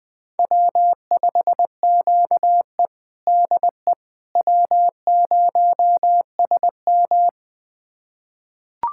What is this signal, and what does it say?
Morse code, heard as a keyed tone, about 20 wpm, 700 Hz. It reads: W5QE DE W0SM
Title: USA Callsign DE Callsign - 4 Characters - 20wpm